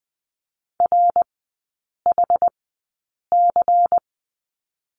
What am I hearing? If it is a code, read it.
Morse code, heard as a keyed tone, about 20 wpm, spaced as for 8 wpm, 700 Hz. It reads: RHC
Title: 107 Lesson 18e - Letter m - Three Characters 20wpm